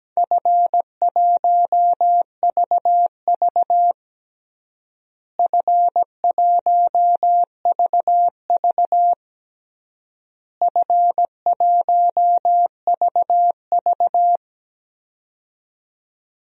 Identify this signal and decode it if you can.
Morse code, heard as a keyed tone, about 17 wpm, 700 Hz. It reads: F1VV F1VV F1VV
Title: DXCC-100 Call Signs - 4 Characters - Repeated 3x - 17wpm